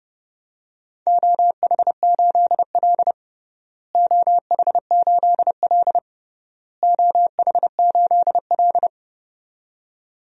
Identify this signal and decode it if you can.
Morse code, heard as a keyed tone, about 30 wpm, 700 Hz. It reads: OH8L OH8L OH8L
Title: DXCC-100 Call Signs - 4 Characters - Repeated 3x - 30wpm